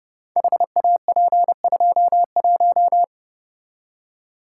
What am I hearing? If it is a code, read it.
Morse code, heard as a keyed tone, about 29 wpm, 700 Hz. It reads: HAP21
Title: Code Groups - 5 Letter:Numbers - 30wpm